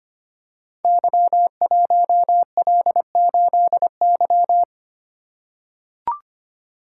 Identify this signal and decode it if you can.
Morse code, heard as a keyed tone, about 25 wpm, 700 Hz. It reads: Y1L8Y
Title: Code Groups - 5 Letter:Numbers - 25wpm